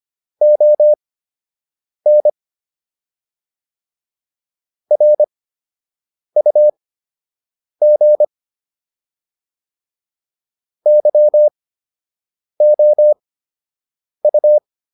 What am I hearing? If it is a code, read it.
Morse code, heard as a keyed tone, about 25 wpm, spaced as for 7 wpm, 600 Hz. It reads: ON RUG YOU